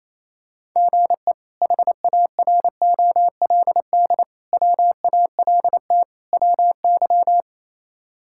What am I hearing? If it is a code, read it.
Morse code, heard as a keyed tone, about 28 wpm, 700 Hz. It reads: GE HAROLD WALT WY
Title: K1USN SST QSO Practice - 28wpm